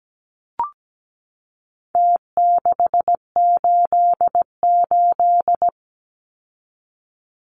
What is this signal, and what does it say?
Morse code, heard as a keyed tone, about 17 wpm, 700 Hz. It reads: T688
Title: Numbers - Contest Style - 4-Digit 17wpm